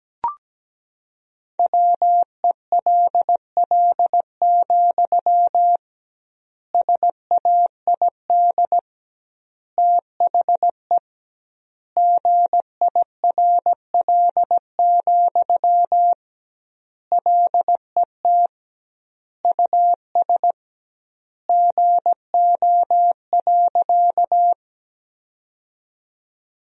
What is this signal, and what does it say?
Morse code, heard as a keyed tone, about 17 wpm, 700 Hz. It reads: WELL, SAID THE GIRL, LET US GO.
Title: 2x Word-Spacing - Sentences from Top 400 Words - 17wpm